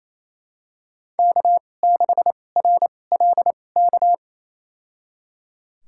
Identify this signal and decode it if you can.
Morse code, heard as a keyed tone, about 28 wpm, spaced as for 20 wpm, 700 Hz. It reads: K6RLK
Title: USA Call Signs - 2x Character Spacing - 28wpm